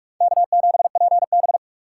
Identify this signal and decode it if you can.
Morse code, heard as a keyed tone, about 56 wpm, 700 Hz. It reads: K7PB